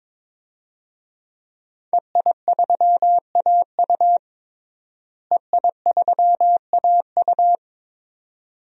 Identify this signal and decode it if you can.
Morse code, heard as a keyed tone, about 22 wpm, 700 Hz. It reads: EI3AU EI3AU
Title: DXCC-100 Call Signs - 5 Characters - Repeated 2x - 22wpm